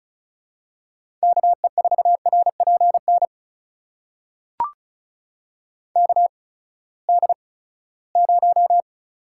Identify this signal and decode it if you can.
Morse code, heard as a keyed tone, about 36 wpm, spaced as for 20 wpm, 700 Hz. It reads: KE4RPN KD0
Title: USA Call Signs - 6 Characters - 8x Character Spacing - 35wpm